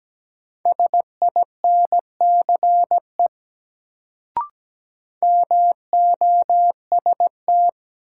Morse code, heard as a keyed tone, about 17 wpm, 700 Hz. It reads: SINCE MOST
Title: Sets of 2 Words - New Word Drill - 17wpm